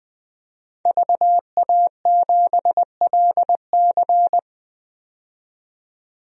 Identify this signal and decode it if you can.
Morse code, heard as a keyed tone, about 21 wpm, 700 Hz. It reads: VA7LC
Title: DXCC-100 Call Signs - 5 Characters - 20wpm